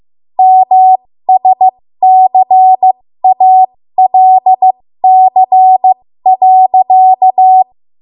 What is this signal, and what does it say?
Morse code, heard as a keyed tone, about 15 wpm, spaced as for 13 wpm, 750 Hz. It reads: MSCALC.
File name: Morse / 13WPM